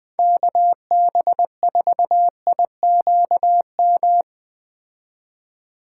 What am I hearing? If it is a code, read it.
Morse code, heard as a keyed tone, about 21 wpm, 700 Hz. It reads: KB4IQM